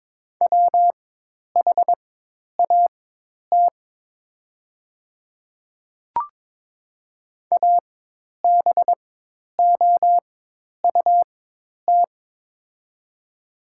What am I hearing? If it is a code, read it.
Morse code, heard as a keyed tone, about 23 wpm, spaced as for 10 wpm, 700 Hz. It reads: WHAT ABOUT
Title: 4x Character Spacing - Top 100 Words - 22wpm